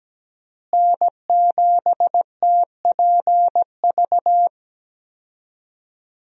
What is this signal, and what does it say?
Morse code, heard as a keyed tone, about 17 wpm, 700 Hz. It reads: N7TPV
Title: USA Call Signs - 5 Characters - Repeated 2x - 17wpm